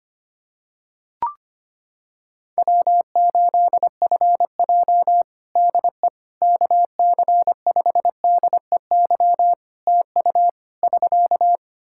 Morse code, heard as a keyed tone, about 25 wpm, 700 Hz. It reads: W8FJ DE KC5DEY TU <SK>
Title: Practice QSO - Exchange 4 - Ending - 25wpm